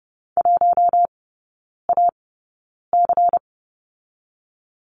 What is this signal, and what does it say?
Morse code, heard as a keyed tone, about 30 wpm, 700 Hz. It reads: 1 A C
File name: Lesson 19e - Letter w - Three Characters 30wpm